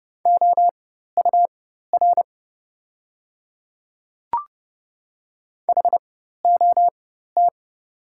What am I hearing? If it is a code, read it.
Morse code, heard as a keyed tone, about 31 wpm, spaced as for 14 wpm, 700 Hz. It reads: OUR HOT
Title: 4x Character Spacing - Top 300 Words - 30wpm